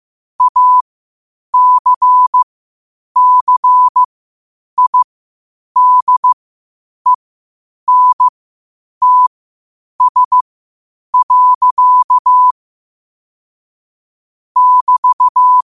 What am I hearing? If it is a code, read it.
Morse code, heard as a keyed tone, about 15 wpm, spaced as for 8 wpm, 1000 Hz. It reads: ACCIDENTS. <BT>
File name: qotdfiles / 8wpm